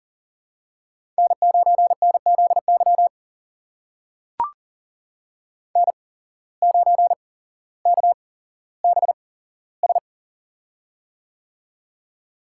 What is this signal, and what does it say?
Morse code, heard as a keyed tone, about 40 wpm, spaced as for 17 wpm, 700 Hz. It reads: N9NZY N9KBS
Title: USA Call Signs - 5 Characters - 8x Character Spacing - 40wpm